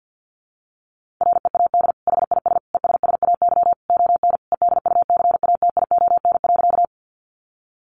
Morse code, heard as a keyed tone, about 50 wpm, 700 Hz. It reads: READ HIS ESSAY ON EDUCATION.
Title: Sentences from Top 1500 Words 50wpm